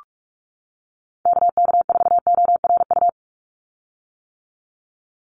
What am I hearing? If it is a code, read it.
Morse code, heard as a keyed tone, about 44 wpm, 700 Hz. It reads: KK4ORU
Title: USA Call Signs 45wpm